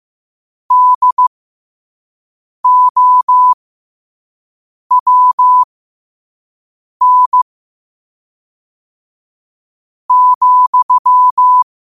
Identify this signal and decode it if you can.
Morse code, heard as a keyed tone, about 15 wpm, spaced as for 5 wpm, 1000 Hz. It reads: DOWN,
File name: qotdfiles / 5wpm